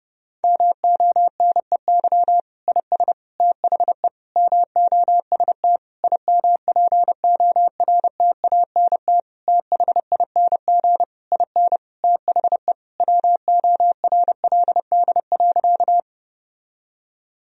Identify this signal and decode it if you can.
Morse code, heard as a keyed tone, about 30 wpm, 700 Hz. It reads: MONEY IS THE MOST IMPORTANT THING IN THE WORLD.
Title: Sentences from Top 1000 Words - Review Part 1 - 30wpm